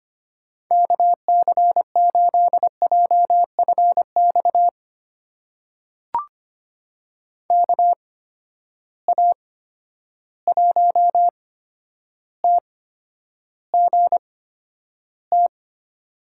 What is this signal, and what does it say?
Morse code, heard as a keyed tone, about 25 wpm, spaced as for 11 wpm, 700 Hz. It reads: KC8JFX KA1TGT
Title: USA Call Signs - 6 Characters - 8x Character Spacing - 25wpm